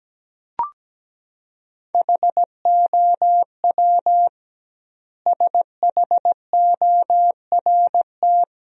Morse code, heard as a keyed tone, about 17 wpm, 700 Hz. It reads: HOW SHORT
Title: 2x Word-Spacing - Sentences from Top 400 Words - 17wpm